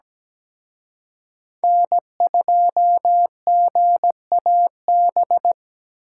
Morse code, heard as a keyed tone, about 20 wpm, 700 Hz. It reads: N2GAB
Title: Speed Racing - USA Call Signs 17wpm